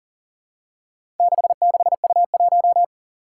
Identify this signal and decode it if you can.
Morse code, heard as a keyed tone, about 38 wpm, 700 Hz. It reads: 66U1
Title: Four Challenging Letter-Number Combinations 40wpm